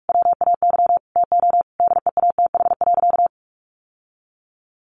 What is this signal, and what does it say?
Morse code, heard as a keyed tone, about 44 wpm, 700 Hz. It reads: WAY TO DEATH.
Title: Sentences from Top 1400 Words 45wpm